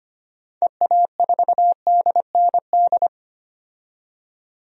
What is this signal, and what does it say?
Morse code, heard as a keyed tone, about 25 wpm, 700 Hz. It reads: EA4DND
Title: DXCC-100 Call Signs - 6 Characters - 25wpm